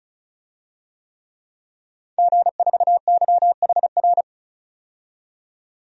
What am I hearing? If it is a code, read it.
Morse code, heard as a keyed tone, about 36 wpm, 700 Hz. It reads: G4YHR
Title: DXCC-100 Call Signs - 5 Characters - 35wpm